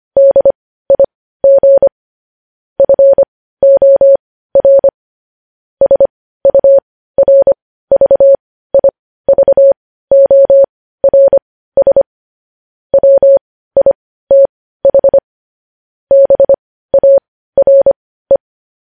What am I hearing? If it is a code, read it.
Morse code, heard as a keyed tone, about 25 wpm, spaced as for 15 wpm, 550 Hz. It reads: DIG FOR SURVIVORS WITH BARE
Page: News Headlines in Morse Code at 15 WPM